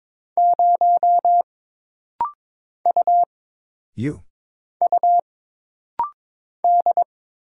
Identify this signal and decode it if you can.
Morse code, heard as a keyed tone, about 22 wpm, 700 Hz. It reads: 0 U U D
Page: This rapid-fire timing is geared to push your ICR (Instant Character Recognition) skills.